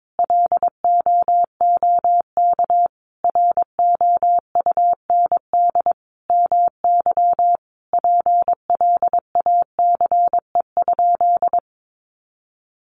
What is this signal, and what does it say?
Morse code, heard as a keyed tone, about 22 wpm, 700 Hz. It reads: LOOK ROUND MY PLACE?
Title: Sentences from Top 200 Words 22wpm